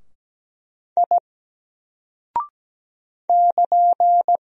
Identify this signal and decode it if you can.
Morse code, heard as a keyed tone, about 17 wpm, 700 Hz. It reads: I <KN>
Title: Single Character (Letter:Number:Punctuation:Prosign) 17wpm